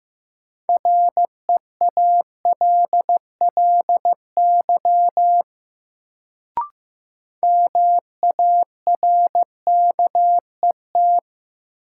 Morse code, heard as a keyed tone, about 15 wpm, 700 Hz. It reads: REALLY MARKET
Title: Sentences from Top 600 Words - New Word Drill - 15wpm